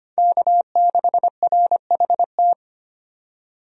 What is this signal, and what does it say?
Morse code, heard as a keyed tone, about 25 wpm, 700 Hz. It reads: K6RHT
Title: USA Call Signs - 5 Characters - 25wpm